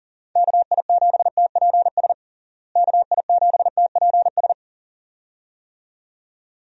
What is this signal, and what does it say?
Morse code, heard as a keyed tone, about 40 wpm, 700 Hz. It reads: KI7TPS KI7TPS
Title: USA Call Signs - 6 Characters - Repeated 2x - 40wpm